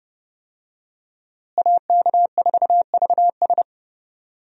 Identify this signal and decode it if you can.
Morse code, heard as a keyed tone, about 30 wpm, 700 Hz. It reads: AK4VS
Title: USA Call Signs - 5 Characters - Repeated 3x - 30wpm